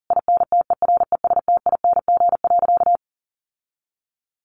Morse code, heard as a keyed tone, about 40 wpm, 700 Hz. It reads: INTERESTING.
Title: Sentences from Top 1000 Words 40wpm